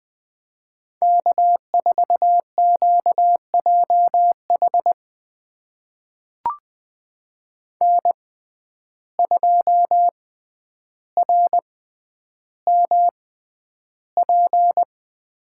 Morse code, heard as a keyed tone, about 20 wpm, spaced as for 10 wpm, 700 Hz. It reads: K4QJH N2RMP
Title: USA Call Signs - 5 Characters - 6x Character Spacing - 20wpm